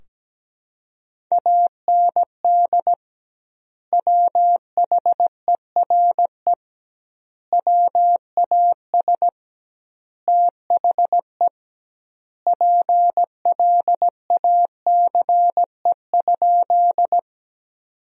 Morse code, heard as a keyed tone, about 17 wpm, 700 Hz. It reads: AND WHERE WAS THE PLACE?
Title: 2x Word-Spacing - Sentences from Top 200 Words - 17wpm